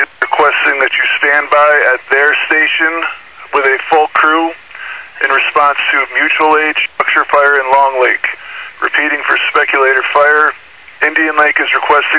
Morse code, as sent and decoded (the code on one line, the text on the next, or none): none